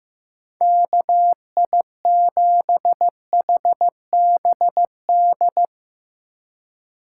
Morse code, heard as a keyed tone, about 15 wpm, 700 Hz. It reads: KI7HBD
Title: USA Call Signs - 6 Characters - 4x Character Spacing - 15wpm